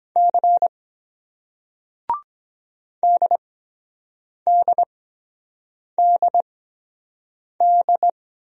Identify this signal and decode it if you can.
Morse code, heard as a keyed tone, about 23 wpm, spaced as for 7 wpm, 700 Hz. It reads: C DDDD